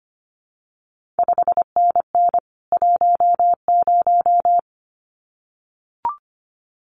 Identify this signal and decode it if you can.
Morse code, heard as a keyed tone, about 25 wpm, 700 Hz. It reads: 5NN 10
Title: CQ WW DX Contest Exchange Encore - 25wpm